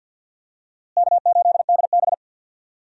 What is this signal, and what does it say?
Morse code, heard as a keyed tone, about 52 wpm, 700 Hz. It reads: K8DB
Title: USA Call Signs - 4 Characters - Repeated 2x - 50wpm